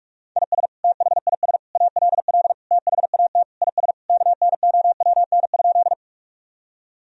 Morse code, heard as a keyed tone, about 43 wpm, 700 Hz. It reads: IS THIS ALL THAT IS KNOWN?
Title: Sentences from Top 1000 Words - Review Part 3 - 45wpm